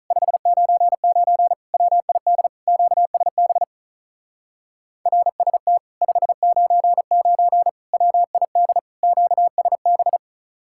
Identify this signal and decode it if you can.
Morse code, heard as a keyed tone, about 44 wpm, 700 Hz. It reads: 599 WID QSB RST 599 WID QSB